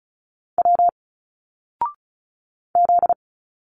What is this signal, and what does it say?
Morse code, heard as a keyed tone, about 35 wpm, 700 Hz. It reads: W Z
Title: Single Letters 35wpm